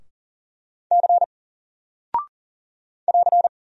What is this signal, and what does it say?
Morse code, heard as a keyed tone, about 39 wpm, 700 Hz. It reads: C <AR>
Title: Single Character (Letter:Number:Punctuation:Prosign) Alt 40wpm